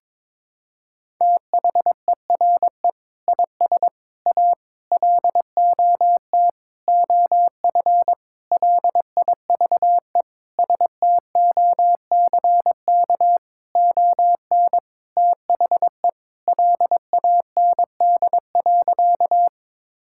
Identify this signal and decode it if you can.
Morse code, heard as a keyed tone, about 22 wpm, 700 Hz. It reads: THERE IS A LOT OF LIVESTOCK ON THE LAND.
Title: Sentences from Top 1000 Words - Review Part 5 - 22wpm